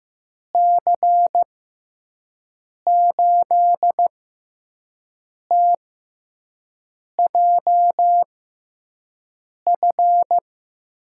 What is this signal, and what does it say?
Morse code, heard as a keyed tone, about 15 wpm, spaced as for 5 wpm, 700 Hz. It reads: C8TJF
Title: USA Call Signs - 6 Characters - 6x Character Spacing - 15wpm